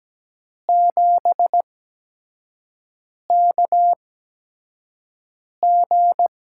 Morse code, heard as a keyed tone, about 17 wpm, spaced as for 5 wpm, 700 Hz. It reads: 7KG
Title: USA Call Signs - 4 Characters - 8x Character Spacing - 17wpm